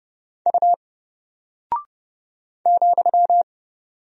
Morse code, heard as a keyed tone, about 30 wpm, 700 Hz. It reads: U ,
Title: Single Character (Letter:Number:Punctuation:Prosign) Alt 30wpm